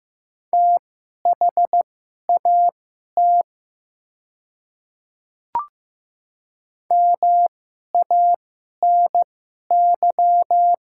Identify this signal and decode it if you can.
Morse code, heard as a keyed tone, about 15 wpm, spaced as for 11 wpm, 700 Hz. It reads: THAT MANY
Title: 2x Character Spacing - Top 300 Words - 15wpm